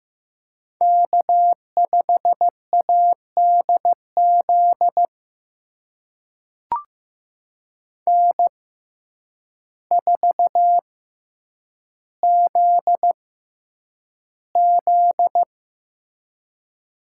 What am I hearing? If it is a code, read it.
Morse code, heard as a keyed tone, about 15 wpm, spaced as for 8 wpm, 700 Hz. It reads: K5ADZ N4ZZ
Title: USA Call Signs - 6x Character Spacing - 15wpm